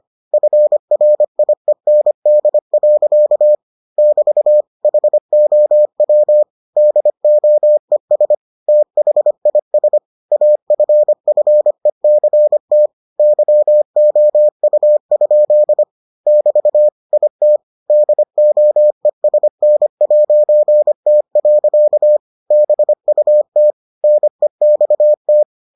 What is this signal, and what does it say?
Morse code, heard as a keyed tone, about 25 wpm, 600 Hz. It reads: FRIEND. <BT> HOW DOES THIS AFFECT YOU? <BT> IT DOESN'T. BUT NEXT